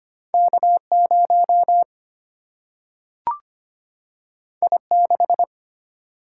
Morse code, heard as a keyed tone, about 25 wpm, 700 Hz. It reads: K 0 I 6
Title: Single-Letter Single-Number 25wpm